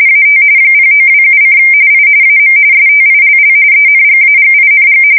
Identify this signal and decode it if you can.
RTTY, about 45.45 baud. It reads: AT 0927: I AM STILL ENGAGING
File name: fsk170 / USS-New-Jersey